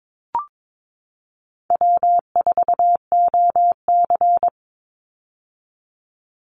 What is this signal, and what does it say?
Morse code, heard as a keyed tone, about 22 wpm, 700 Hz. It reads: W4OC
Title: Easy Call Signs - 22wpm